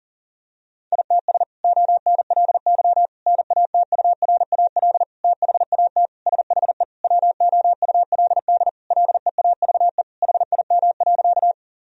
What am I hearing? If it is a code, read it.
Morse code, heard as a keyed tone, about 37 wpm, 700 Hz. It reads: ITS ONLY NATURAL THAT SHE WOULD LEAVE HIM.
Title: Sentences from Top 1000 Words - Review Part 4 - 40wpm